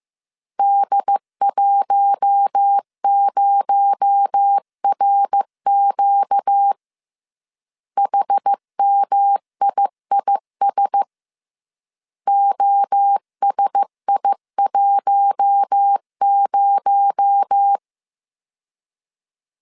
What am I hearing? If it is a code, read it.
Morse code, heard as a keyed tone, about 15 wpm, 800 Hz. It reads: D10RQ HMIIS OSI10